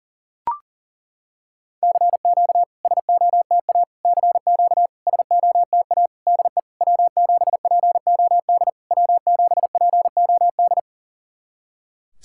Morse code, heard as a keyed tone, about 40 wpm, 700 Hz. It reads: CQ SOTA CQ SOTA DE W7POD W7POD
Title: SOTA QSO Practice 40wpm